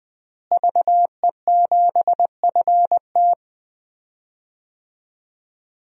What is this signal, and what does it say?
Morse code, heard as a keyed tone, about 20 wpm, 700 Hz. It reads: VE7FT
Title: DXCC-100 Call Signs - 5 Characters - Repeated 2x - 20wpm